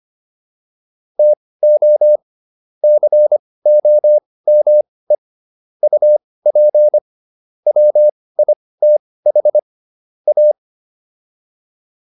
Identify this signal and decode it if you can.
Morse code, heard as a keyed tone, about 25 wpm, spaced as for 18 wpm, 600 Hz. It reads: TO COME UP WITH A